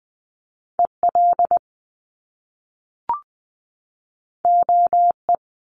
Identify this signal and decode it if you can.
Morse code, heard as a keyed tone, about 20 wpm, 700 Hz. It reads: EL OE
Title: Top Two Letter Combinations in Words 20wpm